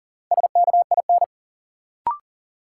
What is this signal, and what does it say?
Morse code, heard as a keyed tone, about 39 wpm, 700 Hz. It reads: SKIN
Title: Sentences from Top 1300 Words - New Word Drill Alt - 40wpm